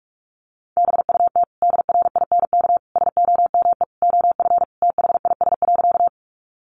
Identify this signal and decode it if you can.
Morse code, heard as a keyed tone, about 44 wpm, 700 Hz. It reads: BUT DRINK SOME OF THIS.
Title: Sentences from Top 1500 Words 45wpm